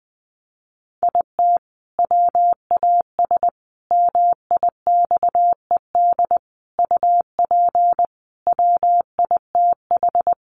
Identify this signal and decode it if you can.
Morse code, heard as a keyed tone, about 20 wpm, 700 Hz. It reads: IT WAS MIXED UP WITH
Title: Sentences from Top 1500 Words 20wpm